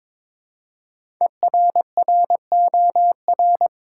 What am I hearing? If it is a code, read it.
Morse code, heard as a keyed tone, about 22 wpm, 700 Hz. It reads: ERROR
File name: 5 Letter Words - First 3 Letters Given - 22wpm